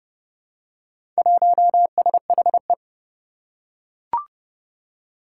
Morse code, heard as a keyed tone, about 31 wpm, 700 Hz. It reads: 1SHE
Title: Four Challenging Letter-Number Combinations 30wpm